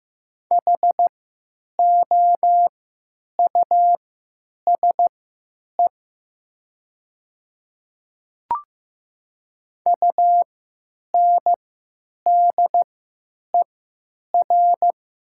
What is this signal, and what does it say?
Morse code, heard as a keyed tone, about 15 wpm, spaced as for 9 wpm, 700 Hz. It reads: HOUSE UNDER
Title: 3x Character Spacing - Top 200 Words - 15wpm